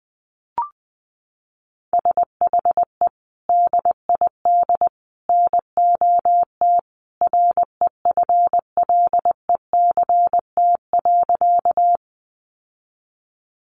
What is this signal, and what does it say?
Morse code, heard as a keyed tone, about 20 wpm, 700 Hz. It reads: SHE DID NOT REFLECT.
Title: Sentences from Top 1500 Words 20wpm